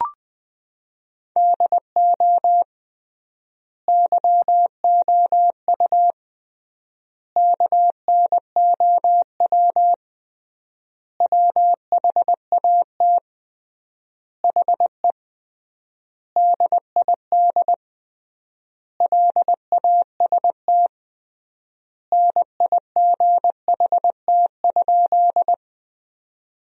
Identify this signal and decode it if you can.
Morse code, heard as a keyed tone, about 21 wpm, 700 Hz. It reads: DO YOU KNOW WHAT HE DID LAST NIGHT?
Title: 3x Word-Spacing - Sentences from Top 300 Words - 20wpm